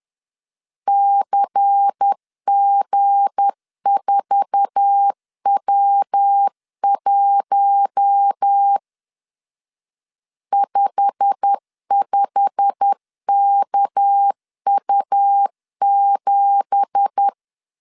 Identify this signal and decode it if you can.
Morse code, heard as a keyed tone, about 10 wpm, 800 Hz. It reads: CG4W1 55KU7
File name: CW / DET / 1 9 1500C S B 10wpm 4439